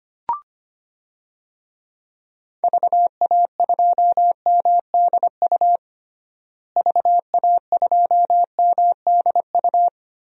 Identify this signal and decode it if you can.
Morse code, heard as a keyed tone, about 25 wpm, 700 Hz. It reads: VA2MDU VA2MDU
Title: DXCC-100 Call Signs - 6 Characters - Repeated 3x - 25wpm